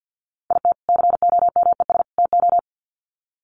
Morse code, heard as a keyed tone, about 50 wpm, 700 Hz. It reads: IT COMES TO
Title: Sets of 3 Words 50wpm